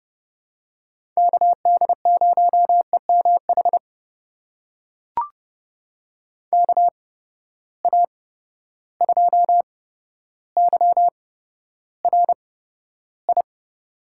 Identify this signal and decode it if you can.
Morse code, heard as a keyed tone, about 31 wpm, spaced as for 13 wpm, 700 Hz. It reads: KD0EMH KA2YRI
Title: USA Call Signs - 6 Characters - 8x Character Spacing - 30wpm